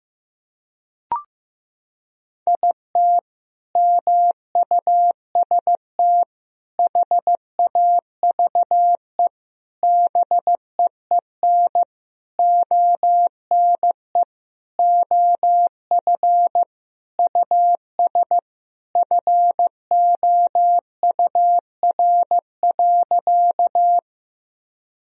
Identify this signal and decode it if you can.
Morse code, heard as a keyed tone, about 15 wpm, 700 Hz. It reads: IT MUST HAVE BEEN ONE OF US FOUR.
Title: Sentences from Top 1000 Words - Review Part 1 - 15wpm